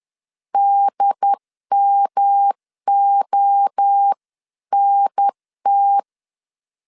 Morse code, heard as a keyed tone, about 10 wpm, 800 Hz. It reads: DMONT
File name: 10wpm